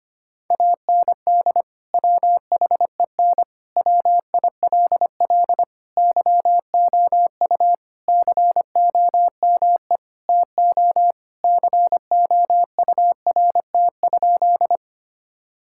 Morse code, heard as a keyed tone, about 25 wpm, 700 Hz. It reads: AND WHEN WILL YOU COME TO COURT?
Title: Sentences from Top 1600 Words 25wpm